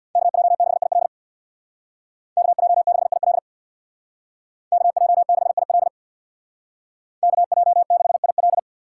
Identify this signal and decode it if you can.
Morse code, heard as a keyed tone, about 62 wpm, 700 Hz. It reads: KJ6IL KJ6IL KJ6IL KJ6IL
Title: Speed Racing - USA 2x2 Call Signs 50wpm